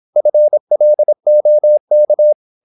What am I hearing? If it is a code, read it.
Morse code, heard as a keyed tone, about 26 wpm, 600 Hz. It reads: FLOK
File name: Korte ord DK 26wpm